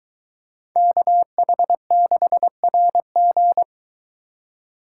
Morse code, heard as a keyed tone, about 23 wpm, 700 Hz. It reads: KH6RG